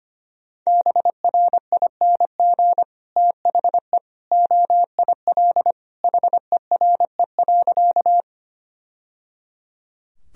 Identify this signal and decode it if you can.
Morse code, heard as a keyed tone, about 25 wpm, 700 Hz. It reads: BRING THE OIL HERE.
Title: Sentences from Top 1000 Words - Review Part 4 - 25wpm